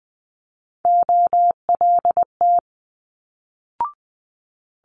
Morse code, heard as a keyed tone, about 20 wpm, 700 Hz. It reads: OLT
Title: Multi-letter Phonemes Quick Fire 20wpm